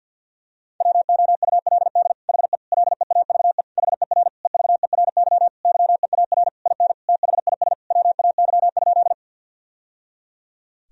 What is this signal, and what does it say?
Morse code, heard as a keyed tone, about 48 wpm, 700 Hz. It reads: WOULD HE LEAVE HER EVERY YEAR IN THIS WAY?
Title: Sentences from Top 1000 Words - Review Part 1 - 50wpm